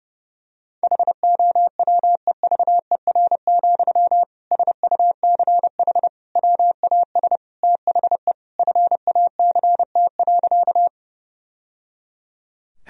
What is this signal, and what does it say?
Morse code, heard as a keyed tone, about 30 wpm, 700 Hz. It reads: HOWEVER, SUCH WAS THE FACT.
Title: Sentences from Top 1000 Words - Review Part 3 - 30wpm